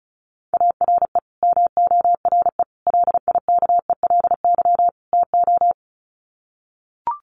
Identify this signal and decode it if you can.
Morse code, heard as a keyed tone, about 34 wpm, 700 Hz. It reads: ARE MORE LIKELY TO
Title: Sets of 4 Words 35wpm